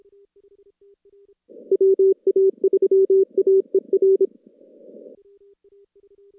Morse code, heard as a keyed tone, about 25 wpm, 390 Hz. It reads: WA3AER
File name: contests / CW-SS-k5tr- -2200